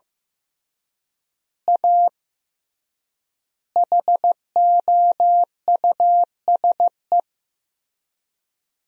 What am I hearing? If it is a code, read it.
Morse code, heard as a keyed tone, about 15 wpm, 700 Hz. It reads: A HOUSE
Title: 3x Word-Spacing - Sentences from Top 200 Words - 15wpm